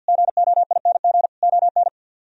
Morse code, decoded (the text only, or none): GOING ON